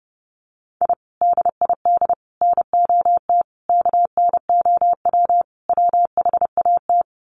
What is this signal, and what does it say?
Morse code, heard as a keyed tone, about 30 wpm, 700 Hz. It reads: I DID NOT KNOW WHAT
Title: Sets of 5 Words 30wpm